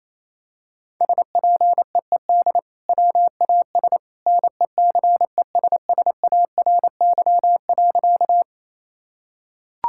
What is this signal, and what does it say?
Morse code, heard as a keyed tone, about 28 wpm, 700 Hz. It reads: SPEED WAS NECESSARY.
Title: Sentences from Top 1000 Words - Review Part 5 - 28wpm